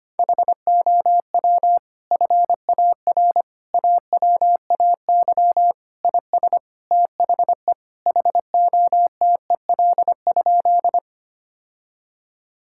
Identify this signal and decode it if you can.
Morse code, heard as a keyed tone, about 25 wpm, 700 Hz. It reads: HOW FAR AWAY IS THE HOTEL?
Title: Sentences from Top 1600 Words 25wpm